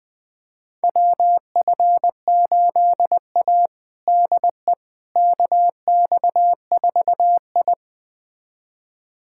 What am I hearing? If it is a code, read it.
Morse code, heard as a keyed tone, about 20 wpm, 700 Hz. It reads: WF8A DE KX4I
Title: USA Callsign DE Callsign - 4 Characters - 20wpm